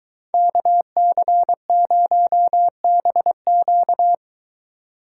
Morse code, heard as a keyed tone, about 23 wpm, 700 Hz. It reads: KC0BQ